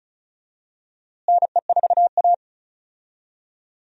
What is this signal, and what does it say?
Morse code, heard as a keyed tone, about 34 wpm, 700 Hz. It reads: NE4A
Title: USA Call Signs - 4 Characters - Repeated 3x - 35wpm